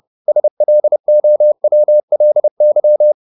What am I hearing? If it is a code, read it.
Morse code, heard as a keyed tone, about 30 wpm, 600 Hz. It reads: SLOWLY